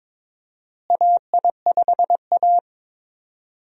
Morse code, heard as a keyed tone, about 21 wpm, 700 Hz. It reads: AI5A
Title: USA Call Signs - 4 Characters - Repeated 2x - 22wpm